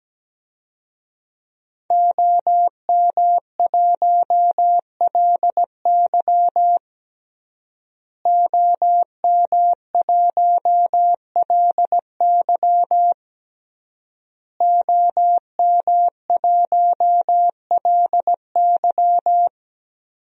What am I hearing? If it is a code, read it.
Morse code, heard as a keyed tone, about 17 wpm, 700 Hz. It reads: OM1LY OM1LY OM1LY
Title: DXCC-100 Call Signs - 5 Characters - Repeated 3x - 17wpm